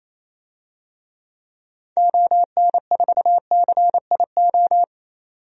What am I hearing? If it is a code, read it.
Morse code, heard as a keyed tone, about 28 wpm, 700 Hz. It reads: ON4CIO
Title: DXCC-100 Call Signs - 6 Characters - 28wpm